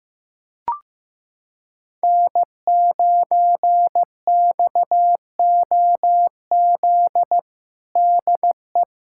Morse code, heard as a keyed tone, about 15 wpm, 700 Hz. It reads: N9XOZ DE
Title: USA Callsign DE Callsign - 5 Characters - 15wpm